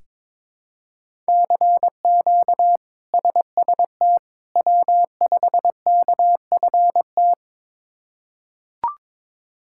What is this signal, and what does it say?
Morse code, heard as a keyed tone, about 22 wpm, 700 Hz. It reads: CQ SST W5KFT
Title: K1USN SST QSO Practice - 22wpm